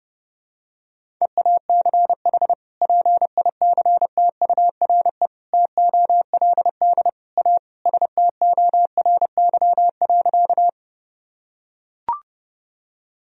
Morse code, heard as a keyed tone, about 30 wpm, 700 Hz. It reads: EACH PICTURE TOLD A STORY.
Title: Sentences from Top 1000 Words - Review Part 1 - 30wpm